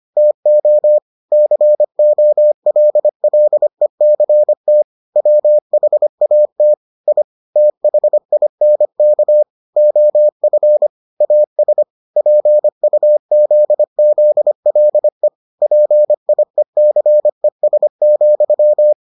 Morse code, heard as a keyed tone, about 25 wpm, 600 Hz. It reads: TO COLLECT WHAT I THINK OF AS PUZZLE PIECES,